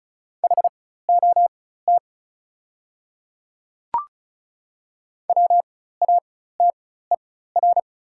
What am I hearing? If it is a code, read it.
Morse code, heard as a keyed tone, about 36 wpm, spaced as for 16 wpm, 700 Hz. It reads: HOT WATER
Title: 4x Character Spacing - Top 300 Words - 35wpm